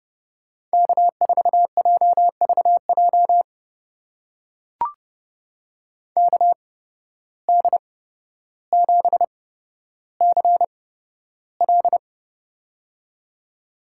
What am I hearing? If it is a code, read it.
Morse code, heard as a keyed tone, about 31 wpm, spaced as for 13 wpm, 700 Hz. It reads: K4JVJ KD7CL
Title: USA Call Signs - 5 Characters - 8x Character Spacing - 30wpm